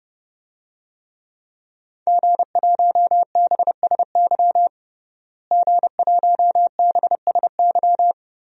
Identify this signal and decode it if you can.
Morse code, heard as a keyed tone, about 29 wpm, 700 Hz. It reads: G1BSY G1BSY
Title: DXCC-100 Call Signs - 5 Characters - Repeated 3x - 30wpm